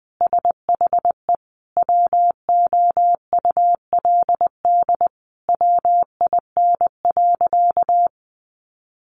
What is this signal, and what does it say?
Morse code, heard as a keyed tone, about 20 wpm, 700 Hz. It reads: SHE WOULD WIN.
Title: Sentences from Top 1500 Words 20wpm